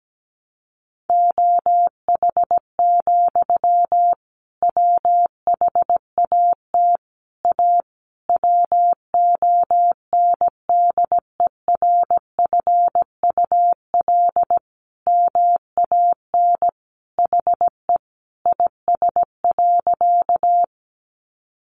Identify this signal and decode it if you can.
Morse code, heard as a keyed tone, about 17 wpm, 700 Hz. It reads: OH, WHAT A WONDERFUL MAN HE IS.
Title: Sentences from Top 1100 Words 17wpm